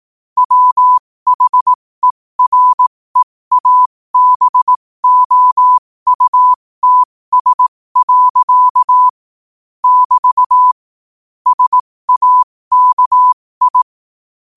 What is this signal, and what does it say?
Morse code, heard as a keyed tone, about 18 wpm, spaced as for 15 wpm, 1000 Hz. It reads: WHEREABOUTS. <BT> SAKI
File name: qotdfiles / 15wpm